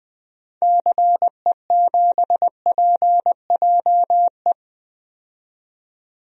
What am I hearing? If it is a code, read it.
Morse code, heard as a keyed tone, about 20 wpm, 700 Hz. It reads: CE7PJE
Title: DXCC-100 Call Signs - 6 Characters - 20wpm